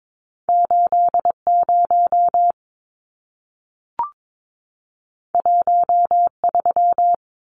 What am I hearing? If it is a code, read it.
Morse code, heard as a keyed tone, about 22 wpm, 700 Hz. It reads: 80 13
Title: Numbers - 2-Digit 22wpm